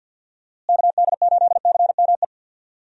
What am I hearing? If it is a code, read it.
Morse code, heard as a keyed tone, about 50 wpm, 700 Hz. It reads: KD8CME
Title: USA Call Signs - 6 Characters - Repeated 3x - 50wpm